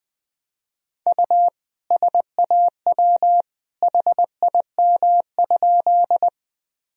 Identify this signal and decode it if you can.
Morse code, heard as a keyed tone, about 21 wpm, 700 Hz. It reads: U SAW HIM?
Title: Sentences from Top 300 Words and Abbreviations 20wpm